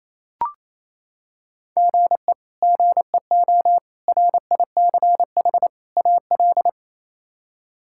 Morse code, heard as a keyed tone, about 28 wpm, 700 Hz. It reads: GE GEO RICH AL
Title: K1USN SST QSO Practice - 28wpm